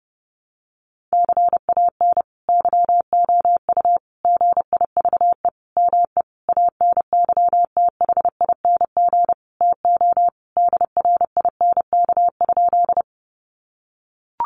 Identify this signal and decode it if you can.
Morse code, heard as a keyed tone, about 30 wpm, 700 Hz. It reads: CAN YOU GIVE ME ANYTHING TO DRINK?
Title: Sentences from Top 1500 Words 30wpm